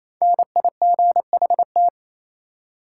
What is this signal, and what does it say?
Morse code, heard as a keyed tone, about 28 wpm, 700 Hz. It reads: NIGHT
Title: Five Letter Words - 28wpm